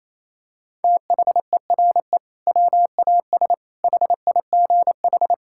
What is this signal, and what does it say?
Morse code, heard as a keyed tone, about 28 wpm, 700 Hz. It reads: THERE WAS HIGH
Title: Sentences from Top 1600 Words 28wpm